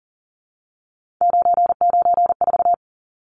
Morse code, heard as a keyed tone, about 40 wpm, 700 Hz. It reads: 994
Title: Numbers - 3-Digit 40wpm